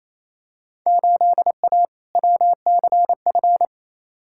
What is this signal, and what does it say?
Morse code, heard as a keyed tone, about 28 wpm, 700 Hz. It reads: 8A WCF
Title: Field Day Exchanges Encore - 28wpm